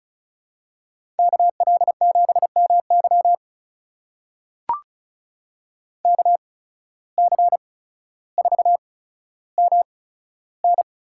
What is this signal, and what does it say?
Morse code, heard as a keyed tone, about 34 wpm, spaced as for 15 wpm, 700 Hz. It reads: KL7MY KC4MN
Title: USA Call Signs - 5 Characters - 8x Character Spacing - 35wpm